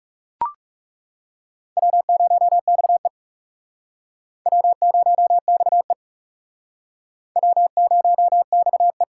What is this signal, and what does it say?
Morse code, heard as a keyed tone, about 37 wpm, 700 Hz. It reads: W0XE W0XE W0XE
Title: Speed Racing - USA 1x2 Call Signs 30wpm